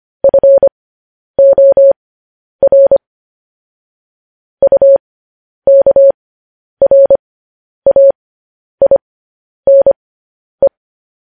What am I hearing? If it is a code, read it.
Morse code, heard as a keyed tone, about 25 wpm, spaced as for 10 wpm, 550 Hz. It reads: FOR UKRAINE